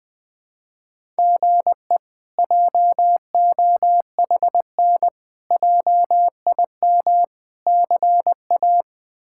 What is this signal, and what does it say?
Morse code, heard as a keyed tone, about 20 wpm, 700 Hz. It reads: GE JOHN JIM CA
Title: K1USN SST QSO Practice - 20wpm